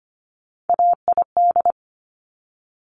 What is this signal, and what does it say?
Morse code, heard as a keyed tone, about 25 wpm, 700 Hz. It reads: AID
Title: Top Three Letter Combinations in Words 25wpm